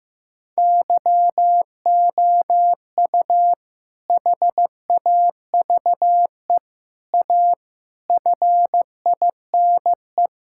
Morse code, heard as a keyed tone, about 15 wpm, 700 Hz. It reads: YOU HAVE A FINE
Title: Sentences from Top 1000 Words - Review Part 6 - 15wpm